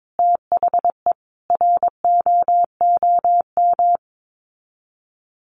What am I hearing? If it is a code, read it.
Morse code, heard as a keyed tone, about 22 wpm, 700 Hz. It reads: THE ROOM
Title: Sets of 5 Words Encore 22wpm